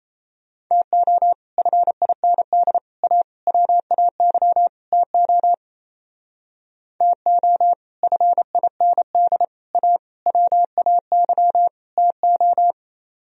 Morse code, heard as a keyed tone, about 35 wpm, 700 Hz. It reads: TO FIND A WAY TO TO FIND A WAY TO
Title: Speed-Racing - Sets of 5 Words 28wpm